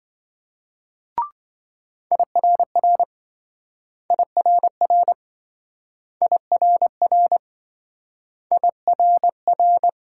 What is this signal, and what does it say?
Morse code, heard as a keyed tone, about 27 wpm, 700 Hz. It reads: IRR IRR IRR IRR